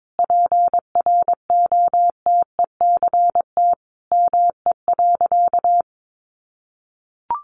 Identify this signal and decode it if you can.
Morse code, heard as a keyed tone, about 22 wpm, 700 Hz. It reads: PROTECT ME.
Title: Sentences from Top 1100 Words 22wpm